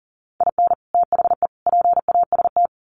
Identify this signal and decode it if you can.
Morse code, heard as a keyed tone, about 40 wpm, 700 Hz. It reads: IN THE PAST
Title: Sets of 3 Words 40wpm